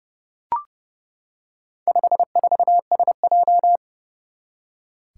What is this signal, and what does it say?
Morse code, heard as a keyed tone, about 30 wpm, 700 Hz. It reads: 54SJ
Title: Four Challenging Letter-Number Combinations 30wpm